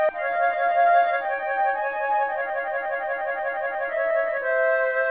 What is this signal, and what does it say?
There is no code.